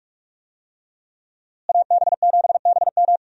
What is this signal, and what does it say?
Morse code, heard as a keyed tone, about 45 wpm, 700 Hz. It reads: AB7BM
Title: USA Call Signs - 5 Characters - Repeated 3x - 45wpm